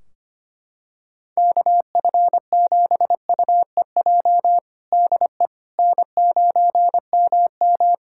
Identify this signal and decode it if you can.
Morse code, heard as a keyed tone, about 25 wpm, 700 Hz. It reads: KF7UEJ DE N9MM
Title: Practice QSO - Exchange 4 - Ending - 25wpm